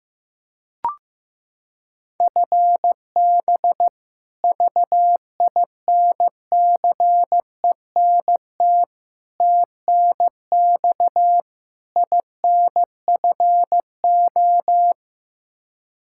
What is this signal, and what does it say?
Morse code, heard as a keyed tone, about 15 wpm, 700 Hz. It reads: FB VINCENT TNX INFO
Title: Practice QSO - Exchange 3 - More Info - 15wpm